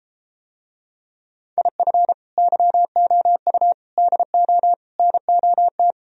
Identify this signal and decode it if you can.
Morse code, heard as a keyed tone, about 33 wpm, 700 Hz. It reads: IF YOU DO NOT